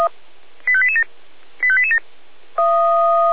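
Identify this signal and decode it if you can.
Five-tone selective call (ZVEI), heard as quick runs of five tones, tones of about 70 ms. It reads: 85887 85887
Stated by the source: Five-tone ZVEI selective call 85884, received on SDR.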